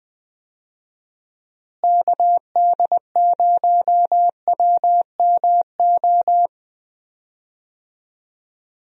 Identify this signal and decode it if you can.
Morse code, heard as a keyed tone, about 20 wpm, 700 Hz. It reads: KD0WMO
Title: USA Call Signs - 6 Characters - Repeated 3x - 20wpm